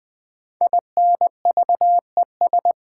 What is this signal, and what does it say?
Morse code, heard as a keyed tone, about 20 wpm, 700 Hz. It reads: INVES
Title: Build A Word - 5 Letters At A Time - 12-Letter Words - 20wpm